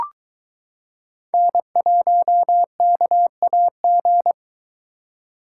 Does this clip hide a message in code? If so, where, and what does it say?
Yes: Morse code from 1.3 to 4.3 s, heard as a keyed tone, about 22 wpm, 700 Hz. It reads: N1KAG